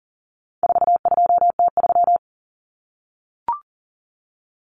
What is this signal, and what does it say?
Morse code, heard as a keyed tone, about 40 wpm, 700 Hz. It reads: 42T3
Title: Common Cut Numbers - 4-Digit 40wpm